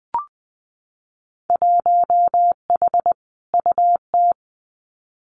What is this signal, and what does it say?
Morse code, heard as a keyed tone, about 19 wpm, 700 Hz. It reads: 1H UT
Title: Winter Field Day Exchanges Encore 20wpm